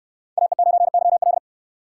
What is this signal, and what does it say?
Morse code, heard as a keyed tone, about 58 wpm, 700 Hz. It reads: AE9YL